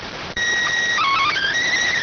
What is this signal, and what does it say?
SSTV (slow-scan television), PD 290.